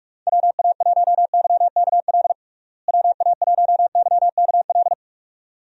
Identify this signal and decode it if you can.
Morse code, heard as a keyed tone, about 46 wpm, 700 Hz. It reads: WA1YKL WA1YKL
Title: USA Call Signs - 6 Characters - Repeated 2x - 45wpm